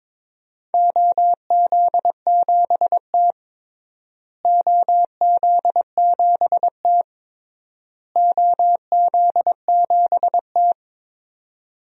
Morse code, heard as a keyed tone, about 22 wpm, 700 Hz. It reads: OZ7T OZ7T OZ7T
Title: DXCC-100 Call Signs - 4 Characters - Repeated 3x - 22wpm